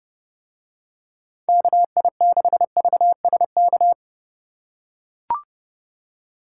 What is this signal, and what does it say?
Morse code, heard as a keyed tone, about 28 wpm, 700 Hz. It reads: KI6VSK